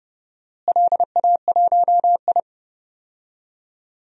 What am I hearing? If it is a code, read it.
Morse code, heard as a keyed tone, about 32 wpm, 700 Hz. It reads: LA1I
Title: DXCC-100 Call Signs - 4 Characters - Repeated 3x - 30wpm